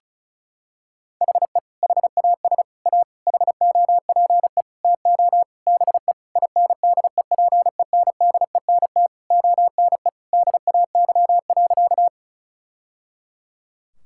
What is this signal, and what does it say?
Morse code, heard as a keyed tone, about 34 wpm, 700 Hz. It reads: HE HAS A HOPE TO BE INDEPENDENT ONE DAY.
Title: Sentences from Top 1000 Words - Review Part 6 - 35wpm